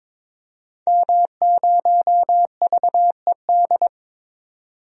Morse code, heard as a keyed tone, about 22 wpm, 700 Hz. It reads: M0VED
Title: DXCC-100 Call Signs - 5 Characters - 22wpm